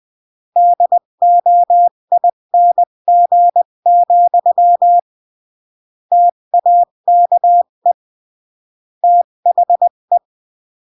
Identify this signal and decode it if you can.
Morse code, heard as a keyed tone, about 20 wpm, 700 Hz. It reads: DOING, TAKE THE
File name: CW / jodorowsky / 82 commandments 20-15